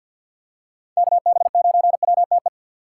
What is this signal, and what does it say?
Morse code, heard as a keyed tone, about 53 wpm, 700 Hz. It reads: KB9WTE
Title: USA Call Signs - 6 Characters - 6x Character Spacing - 50wpm